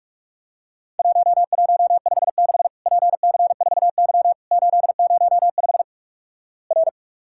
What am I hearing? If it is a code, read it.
Morse code, heard as a keyed tone, about 46 wpm, 700 Hz. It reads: 1156 PC4Y 805 R
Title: Worked All Europe DX Contest - QTC Exchange - 45wpm